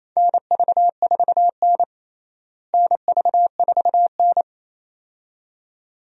Morse code, heard as a keyed tone, about 28 wpm, 700 Hz. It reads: NV4N NV4N
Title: USA Call Signs - 4 Characters - Repeated 2x - 28wpm